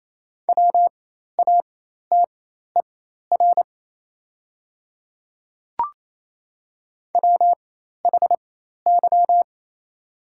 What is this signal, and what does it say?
Morse code, heard as a keyed tone, about 28 wpm, spaced as for 13 wpm, 700 Hz. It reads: WATER WHY
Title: 4x Character Spacing - Top 200 Words - 28wpm